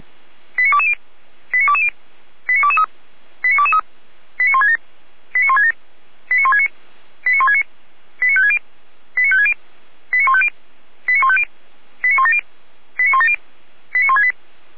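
Five-tone selective call (ZVEI), heard as quick runs of five tones, tones of about 70 ms. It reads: 89220 89220 89223 89223 89167 89167 89169 89169 89560 89560 89150 89150 89170 89170 89168
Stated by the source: ZVEI five-tone call received by SDR.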